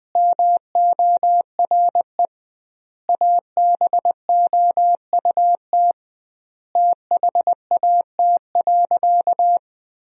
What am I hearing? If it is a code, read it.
Morse code, heard as a keyed tone, about 20 wpm, 700 Hz. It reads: MORE ABOUT THAT.
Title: 2x Word-Spacing - Sentences from Top 200 Words - 20wpm